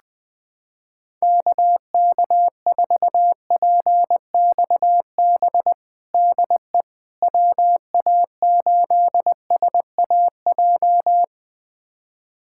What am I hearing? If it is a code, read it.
Morse code, heard as a keyed tone, about 20 wpm, 700 Hz. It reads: KK4PXB DE WA8SAJ
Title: USA Callsign DE Callsign - 6 Characters - 20wpm